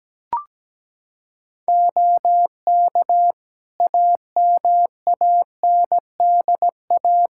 Morse code, heard as a keyed tone, about 17 wpm, 700 Hz. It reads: OK AMANDA
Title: K1USN SST QSO Practice 17wpm